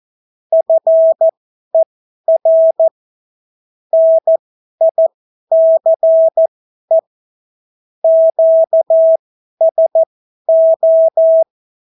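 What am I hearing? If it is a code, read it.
Morse code, heard as a keyed tone, about 14 wpm, spaced as for 11 wpm, 650 Hz. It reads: FER NICE QSO
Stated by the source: Example Spanish CW QSO from a Morse learning book, 14 wpm characters with 8 wpm spacing.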